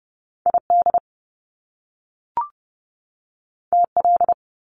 Morse code, heard as a keyed tone, about 30 wpm, 700 Hz. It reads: ID TL
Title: Top Two Letter Combinations in Words 30wpm